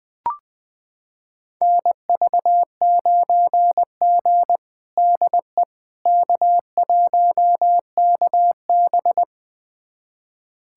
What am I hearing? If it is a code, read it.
Morse code, heard as a keyed tone, about 21 wpm, 700 Hz. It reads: NV9G DE K1KB
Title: USA Callsign DE Callsign - 4 Characters - 20wpm